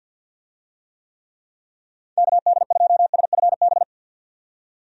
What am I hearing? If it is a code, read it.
Morse code, heard as a keyed tone, about 51 wpm, 700 Hz. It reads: KD2SFB
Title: USA Call Signs - 6 Characters - 50wpm